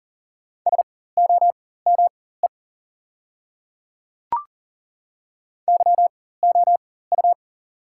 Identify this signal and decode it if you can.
Morse code, heard as a keyed tone, about 40 wpm, spaced as for 19 wpm, 700 Hz. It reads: SOME YOU
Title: 4x Character Spacing - Top 200 Words - 40wpm